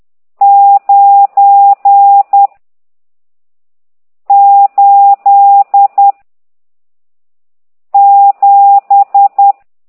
Morse code, heard as a keyed tone, about 10 wpm, 800 Hz. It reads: 9 8 7
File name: ct2idw / exerc14